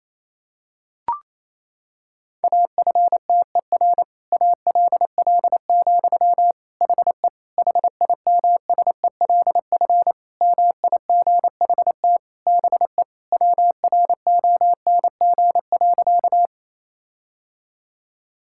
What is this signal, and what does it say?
Morse code, heard as a keyed tone, about 27 wpm, 700 Hz. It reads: AFTER ALL, HE HIMSELF MIGHT BE WRONG.
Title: Sentences from Top 1600 Words 28wpm